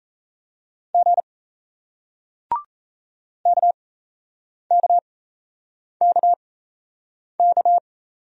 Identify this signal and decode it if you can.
Morse code, heard as a keyed tone, about 35 wpm, spaced as for 8 wpm, 700 Hz. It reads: G KKKK
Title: Speed-Racing Single Letters 28wpm